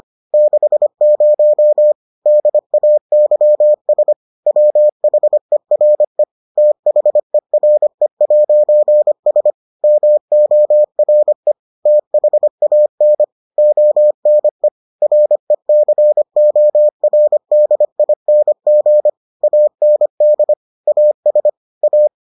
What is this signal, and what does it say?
Morse code, heard as a keyed tone, about 25 wpm, 600 Hz. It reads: 60 DAYS WHERE THERE'S MORE THAN ONE RECORDING AND AS A